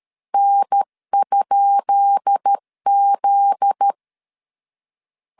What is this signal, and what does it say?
Morse code, heard as a keyed tone, about 13 wpm, 800 Hz. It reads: N?Z
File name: LEZIONI / 10 SIGNS / 10B INT 12wpm 2414